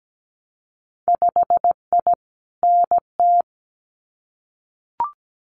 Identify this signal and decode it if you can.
Morse code, heard as a keyed tone, about 17 wpm, 700 Hz. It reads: 5I NT
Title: Winter Field Day Exchanges Encore 17wpm